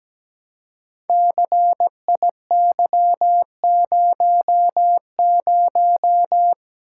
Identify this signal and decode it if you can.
Morse code, heard as a keyed tone, about 17 wpm, 700 Hz. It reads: CIY00
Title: Code Groups - 5 Letter:Numbers - 17wpm